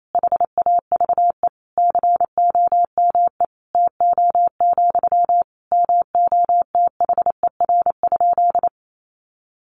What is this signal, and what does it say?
Morse code, heard as a keyed tone, about 28 wpm, 700 Hz. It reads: HAVE COME TO, MOTHER?
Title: Sentences from Top 1000 Words 28wpm